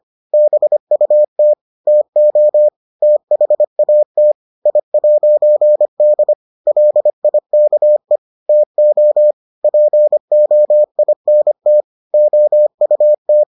Morse code, heard as a keyed tone, about 25 wpm, 600 Hz. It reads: BUT TO THAT I'D LIKE TO POINT OUT